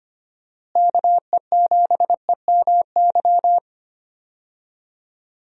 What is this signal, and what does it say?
Morse code, heard as a keyed tone, about 25 wpm, 700 Hz. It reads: KE7EMY
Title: USA Call Signs - 6 Characters - Repeated 3x - 25wpm